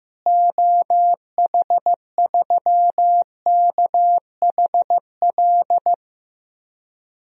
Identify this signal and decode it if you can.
Morse code, heard as a keyed tone, about 15 wpm, 700 Hz. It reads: OH3KHL
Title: DXCC-100 Call Signs - 6 Characters - Repeated 3x - 15wpm